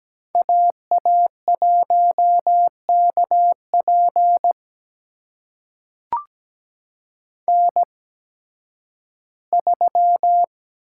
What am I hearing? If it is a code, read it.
Morse code, heard as a keyed tone, about 17 wpm, spaced as for 11 wpm, 700 Hz. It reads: AA1KP N3
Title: USA Call Signs - 5 Characters - 8x Character Spacing - 17wpm